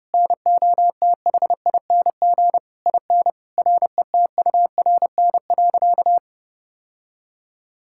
Morse code, heard as a keyed tone, about 32 wpm, 700 Hz. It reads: NOTHING IN RETURN.
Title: Sentences from Top 1000 Words - Review Part 6 - 30wpm